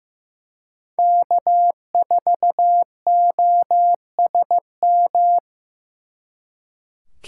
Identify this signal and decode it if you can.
Morse code, heard as a keyed tone, about 18 wpm, 700 Hz. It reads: K4OSM
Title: Speed Racing - USA 1x3 Call Signs 15wpm